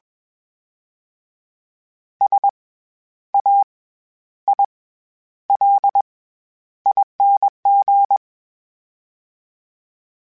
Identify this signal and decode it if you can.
Morse code, heard as a keyed tone, about 21 wpm, 800 Hz. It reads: S A I L ING
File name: Special INGs-mp3s / Special INGs-mp3s / SlowToFaster ing @20